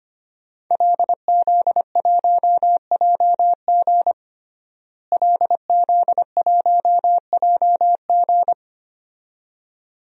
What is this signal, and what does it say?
Morse code, heard as a keyed tone, about 25 wpm, 700 Hz. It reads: LZ1JG LZ1JG
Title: DXCC-100 Call Signs - 5 Characters - Repeated 2x - 25wpm